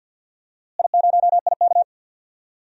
Morse code, heard as a keyed tone, about 52 wpm, 700 Hz. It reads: I0IK
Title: DXCC-100 Call Signs - 4 Characters - Repeated 2x - 50wpm